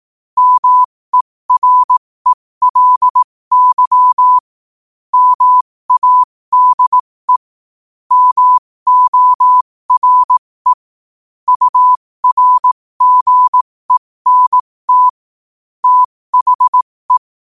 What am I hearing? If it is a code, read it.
Morse code, heard as a keyed tone, about 18 wpm, spaced as for 15 wpm, 1000 Hz. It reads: MERELY MADE MORE URGENT THE
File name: qotdfiles / 15wpm